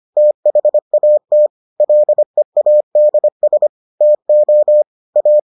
Morse code, heard as a keyed tone, about 25 wpm, 600 Hz. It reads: THAT LEADS TO A